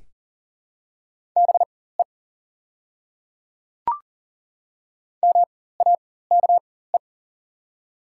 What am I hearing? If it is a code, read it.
Morse code, heard as a keyed tone, about 42 wpm, spaced as for 19 wpm, 690 Hz. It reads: BE MAKE
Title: 4x Character Spacing - Top 100 Words - 40wpm